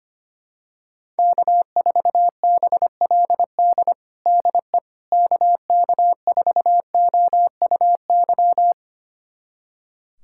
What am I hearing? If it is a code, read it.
Morse code, heard as a keyed tone, about 25 wpm, 700 Hz. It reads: K4BLD DE KK4OUY
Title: USA Callsign DE Callsign 25wpm